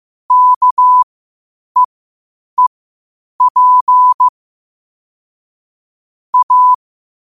Morse code, heard as a keyed tone, about 15 wpm, spaced as for 8 wpm, 1000 Hz. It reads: KEEP A
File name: qotdfiles / 8wpm